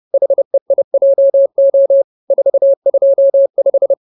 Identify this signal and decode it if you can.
Morse code, heard as a keyed tone, about 30 wpm, 550 Hz. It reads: HEIJO 425
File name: CWT-213-30